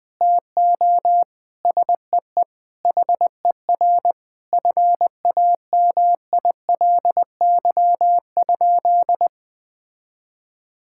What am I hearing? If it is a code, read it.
Morse code, heard as a keyed tone, about 20 wpm, 700 Hz. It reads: TO SEE HER FAMILY?
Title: Sentences from Top 1000 Words - Review Part 6 - 20wpm